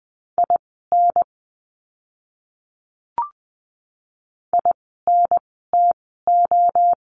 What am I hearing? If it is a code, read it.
Morse code, heard as a keyed tone, about 20 wpm, spaced as for 14 wpm, 700 Hz. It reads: IN INTO
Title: 237 Lesson 38f - Letter q - Words 20wpm